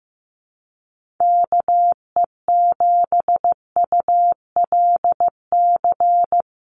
Morse code, heard as a keyed tone, about 15 wpm, 700 Hz. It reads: KE7ULC
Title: USA Call Signs Repeated 2x 15wpm